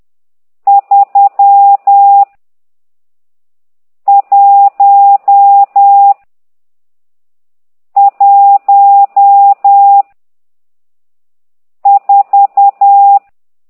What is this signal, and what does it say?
Morse code, heard as a keyed tone, about 10 wpm, 800 Hz. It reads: 3 1 1 4
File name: ct2idw / exerc12